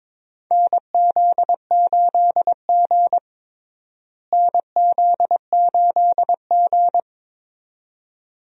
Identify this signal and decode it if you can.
Morse code, heard as a keyed tone, about 21 wpm, 700 Hz. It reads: NZ8G NZ8G
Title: USA Call Signs - 4 Characters - Repeated 2x - 22wpm